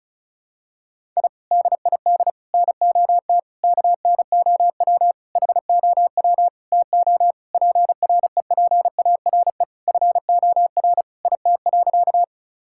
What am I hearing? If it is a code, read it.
Morse code, heard as a keyed tone, about 36 wpm, 700 Hz. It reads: I DID NOT KNOW HOW TO PREPARE FOR IT.
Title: Sentences from Top 1000 Words - Review Part 6 - 35wpm